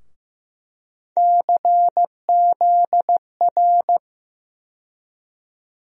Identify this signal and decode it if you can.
Morse code, heard as a keyed tone, about 15 wpm, 700 Hz. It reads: CZR
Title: USA Call Sign Postfixes - 3 Letters 15wpm